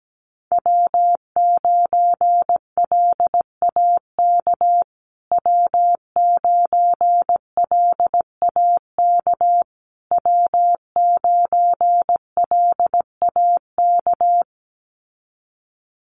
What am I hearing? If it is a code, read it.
Morse code, heard as a keyed tone, about 17 wpm, 700 Hz. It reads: W9LAK W9LAK W9LAK
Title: USA Call Signs Repeated 3x 17wpm